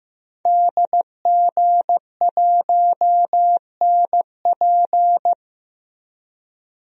Morse code, heard as a keyed tone, about 16 wpm, 700 Hz. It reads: DG1NP
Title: DXCC-100 Call Signs - 5 Characters - Repeated 2x - 15wpm